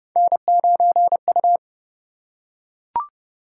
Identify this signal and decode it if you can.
Morse code, heard as a keyed tone, about 27 wpm, 700 Hz. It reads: N9U